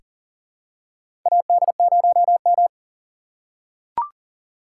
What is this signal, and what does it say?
Morse code, heard as a keyed tone, about 40 wpm, 700 Hz. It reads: AD0M
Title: USA Call Signs - 4 Characters - 3x Character Spacing - 40wpm